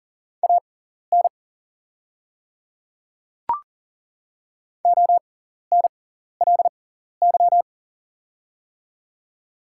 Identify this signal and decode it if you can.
Morse code, heard as a keyed tone, about 40 wpm, spaced as for 14 wpm, 700 Hz. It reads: AN ONLY
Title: 6x Character Spacing - Top 300 Words - 40wpm